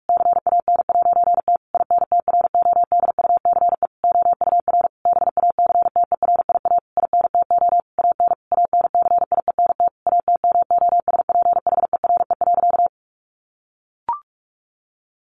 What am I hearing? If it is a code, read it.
Morse code, heard as a keyed tone, about 44 wpm, 700 Hz. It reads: CAN'T INTRODUCE OUR BACTERIA INTO AN ANCIENT ATMOSPHERE.
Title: A Sound Of Thunder 45wpm